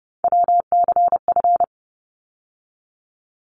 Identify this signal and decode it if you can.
Morse code, heard as a keyed tone, about 30 wpm, 700 Hz. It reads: WCF
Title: ARRL Sections 30wpm